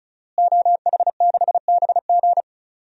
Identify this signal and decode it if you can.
Morse code, heard as a keyed tone, about 36 wpm, 700 Hz. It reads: OH6BG
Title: DXCC-100 Call Signs - 5 Characters - Repeated 2x - 35wpm